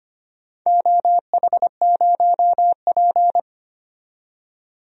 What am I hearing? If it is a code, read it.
Morse code, heard as a keyed tone, about 25 wpm, 700 Hz. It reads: OH0P
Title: DXCC-100 Call Signs - 4 Characters - 25wpm